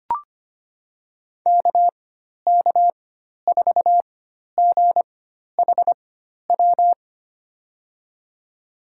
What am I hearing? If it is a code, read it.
Morse code, heard as a keyed tone, about 25 wpm, spaced as for 12 wpm, 700 Hz. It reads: KK4GHW
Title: USA Call Signs - 4x Character Spacing - 25wpm